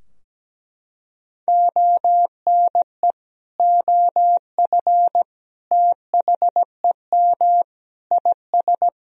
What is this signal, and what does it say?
Morse code, heard as a keyed tone, about 17 wpm, 700 Hz. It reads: ONE OF THEM IS
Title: Sentences from Top 1000 Words - Review Part 4 - 17wpm